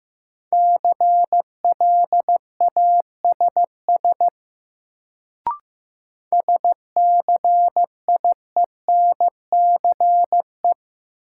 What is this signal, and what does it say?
Morse code, heard as a keyed tone, about 15 wpm, 700 Hz. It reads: CLASS SCIENCE
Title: Sentences from Top 1400 Words - New Word Drill Alt - 15wpm